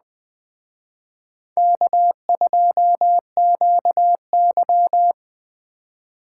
Morse code, heard as a keyed tone, about 20 wpm, 700 Hz. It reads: K2QY
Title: Speed Racing - USA Call Signs 17wpm